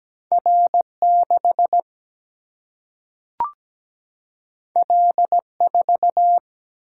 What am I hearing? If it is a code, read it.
Morse code, heard as a keyed tone, about 17 wpm, 700 Hz. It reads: R6 L4
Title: Single-Letter Single-Number 17wpm